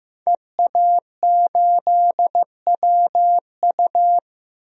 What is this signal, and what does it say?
Morse code, heard as a keyed tone, about 15 wpm, 700 Hz. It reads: EA8WU
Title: DXCC-100 Call Signs - 5 Characters - 15wpm